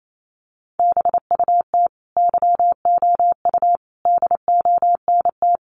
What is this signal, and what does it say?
Morse code, heard as a keyed tone, about 28 wpm, 700 Hz. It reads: BUT YOU DONT
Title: Sentences from Top 600 Words 28wpm